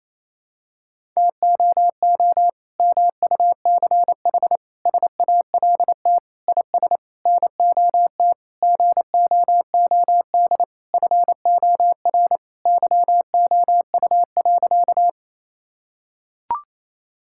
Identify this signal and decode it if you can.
Morse code, heard as a keyed tone, about 28 wpm, 700 Hz. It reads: TOO MUCH SALT IS NOT GOOD FOR YOU.
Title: Sentences from Top 1000 Words - Review Part 5 - 28wpm